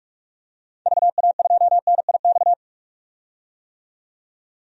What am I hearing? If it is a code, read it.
Morse code, heard as a keyed tone, about 47 wpm, 700 Hz. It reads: VA2NIX
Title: DXCC-100 Call Signs - 6 Characters - 45wpm